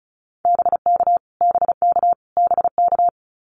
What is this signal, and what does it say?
Morse code, heard as a keyed tone, about 34 wpm, 700 Hz. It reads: BK BK BK
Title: Abbreviations Repeated 3x - 35wpm